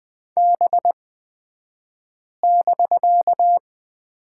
Morse code, heard as a keyed tone, about 20 wpm, spaced as for 5 wpm, 700 Hz. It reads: B<BK>
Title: 249 Lesson 40d - BK - Two Characters 20wpm